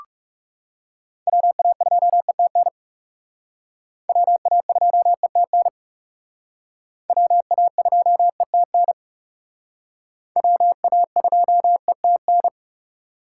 Morse code, heard as a keyed tone, about 36 wpm, 700 Hz. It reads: WA2ETN WA2ETN WA2ETN WA2ETN
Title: Speed Racing - USA 2x3 Call Signs 30wpm